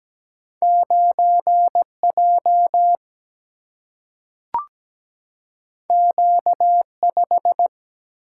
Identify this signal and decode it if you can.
Morse code, heard as a keyed tone, about 17 wpm, 700 Hz. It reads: 9J Q5
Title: Single-Letter Single-Number 17wpm